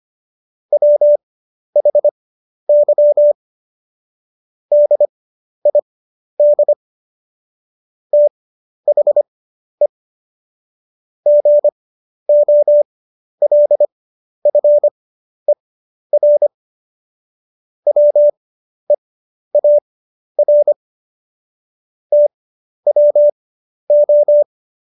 Morse code, heard as a keyed tone, about 25 wpm, spaced as for 11 wpm, 590 Hz. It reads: WHY DID THE GOLFER WEAR TWO
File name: Jokes-25-char-10-eff